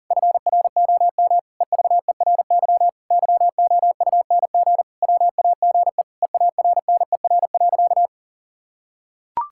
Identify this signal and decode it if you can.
Morse code, heard as a keyed tone, about 40 wpm, 700 Hz. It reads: FROM EVERY YOUNG WAGE EARNER.
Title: Pangram Challenge v2 - 40wpm